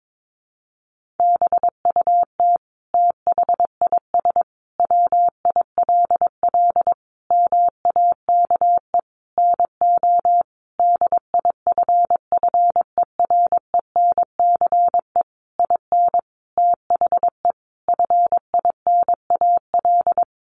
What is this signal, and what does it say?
Morse code, heard as a keyed tone, about 22 wpm, 700 Hz. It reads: BUT THIS WILL MAKE NO DIFFERENCE IN THE FINAL
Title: Sentences from Top 800 Words 22wpm